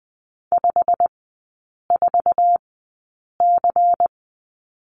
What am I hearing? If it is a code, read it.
Morse code, heard as a keyed tone, about 20 wpm, spaced as for 8 wpm, 690 Hz. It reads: 54C
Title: 093 Lesson 16e - Letter c - Three Characters 20wpm